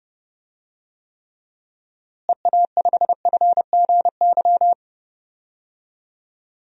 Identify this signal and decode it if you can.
Morse code, heard as a keyed tone, about 32 wpm, 700 Hz. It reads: EA5FGY
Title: DXCC-100 Call Signs - 6 Characters - Repeated 2x - 30wpm